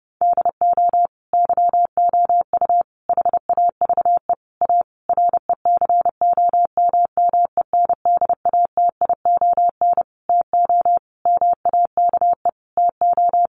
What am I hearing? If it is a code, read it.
Morse code, heard as a keyed tone, about 30 wpm, 700 Hz. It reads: DO YOU HAVE A RECOMMENDATION TO MAKE TO
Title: Sentences from Top 1400 Words 30wpm